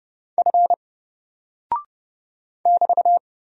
Morse code, heard as a keyed tone, about 30 wpm, 700 Hz. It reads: F <BT>
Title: Single Character (Letter:Number:Punctuation:Prosign) 30wpm